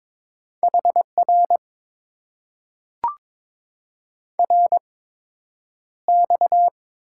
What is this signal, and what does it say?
Morse code, heard as a keyed tone, about 22 wpm, spaced as for 9 wpm, 700 Hz. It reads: HR RX
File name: Top 90 QSO Elements - 8x Character Spacing - 22wpm